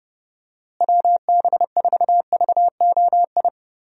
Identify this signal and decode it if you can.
Morse code, heard as a keyed tone, about 32 wpm, 700 Hz. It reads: WB4VOI
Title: USA Call Signs - 6 Characters - Repeated 3x - 30wpm